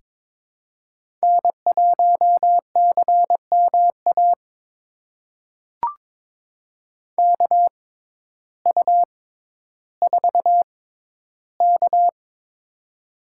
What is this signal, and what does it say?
Morse code, heard as a keyed tone, about 22 wpm, spaced as for 12 wpm, 700 Hz. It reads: N1CMA KU4K
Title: USA Call Signs - 6x Character Spacing - 22wpm